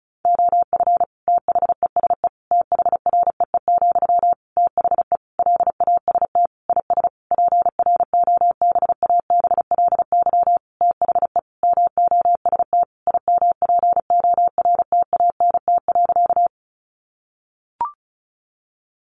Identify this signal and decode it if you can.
Morse code, heard as a keyed tone, about 34 wpm, 700 Hz. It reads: OF THESE THREE, THE LAST IS PROBABLY THE MOST IMPORTANT.
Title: Sentences from Top 700 Words 35wpm